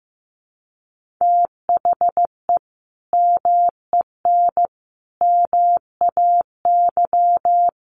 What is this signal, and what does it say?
Morse code, heard as a keyed tone, about 15 wpm, 700 Hz. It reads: THE MEN MAY
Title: Sentences from Top 1500 Words 15wpm